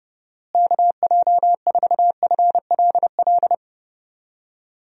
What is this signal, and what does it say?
Morse code, heard as a keyed tone, about 32 wpm, 700 Hz. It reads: KJ4FLL
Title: USA Call Signs - 6 Characters - Repeated 3x - 30wpm